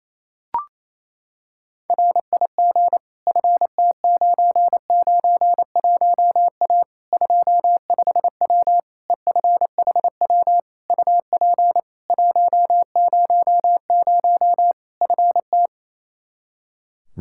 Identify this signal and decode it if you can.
Morse code, heard as a keyed tone, about 28 wpm, 700 Hz. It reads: RIG FT991A 25W EFHW UP 100 FT
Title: Practice QSO - Exchange 2 - Info - 28wpm